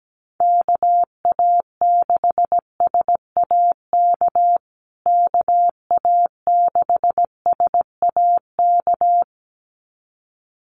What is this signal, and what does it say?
Morse code, heard as a keyed tone, about 17 wpm, 700 Hz. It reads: KA6SAK KA6SAK
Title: USA Call Signs Repeated 2x 17wpm